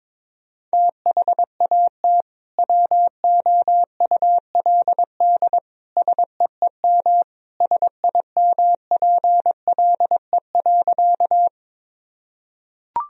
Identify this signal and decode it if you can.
Morse code, heard as a keyed tone, about 22 wpm, 700 Hz. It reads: THAT WOULD SEEM SIMPLE.
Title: Sentences from Top 1000 Words - Review Part 1 - 22wpm